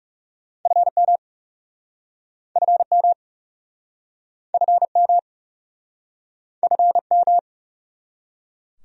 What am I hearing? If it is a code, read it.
Morse code, heard as a keyed tone, about 40 wpm, 700 Hz. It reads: FM FM FM FM